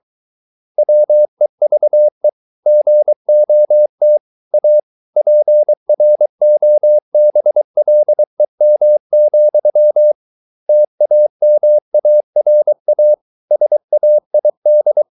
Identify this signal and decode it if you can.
Morse code, heard as a keyed tone, about 23 wpm, 600 Hz. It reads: WEVE GOT A PROBLEM, TAMARA SAID